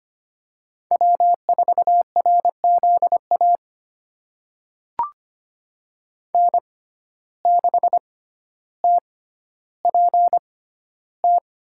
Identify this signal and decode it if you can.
Morse code, heard as a keyed tone, about 25 wpm, spaced as for 13 wpm, 700 Hz. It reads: W4RZA N6TPT
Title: USA Call Signs - 5 Characters - 6x Character Spacing - 25wpm